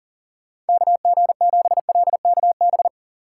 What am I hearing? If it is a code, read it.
Morse code, heard as a keyed tone, about 52 wpm, 700 Hz. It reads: KG7LKB